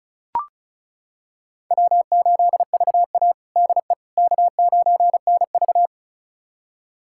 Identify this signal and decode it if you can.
Morse code, heard as a keyed tone, about 36 wpm, 700 Hz. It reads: W8VA DE K9NV
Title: USA Callsign DE Callsign - 4 Characters - 35wpm